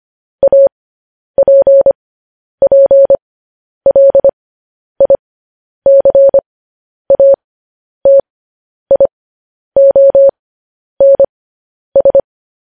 Morse code, heard as a keyed tone, about 25 wpm, spaced as for 10 wpm, 550 Hz. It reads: APPLICATIONS
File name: podcast / News headlines-1x-25x10